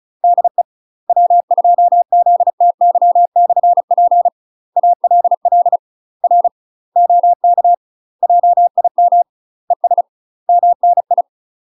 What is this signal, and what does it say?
Morse code, heard as a keyed tone, about 34 wpm, 700 Hz. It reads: DE W2ZTY/P ALL R OK JIM ES MNI
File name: UK MASTER QSOs TRAINING / 35WPM / UK MASTER QSOs 0039